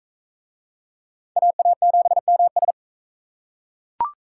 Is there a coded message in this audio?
Yes: Morse code from 1.4 to 2.7 s, heard as a keyed tone, about 37 wpm, 700 Hz. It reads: AA7MS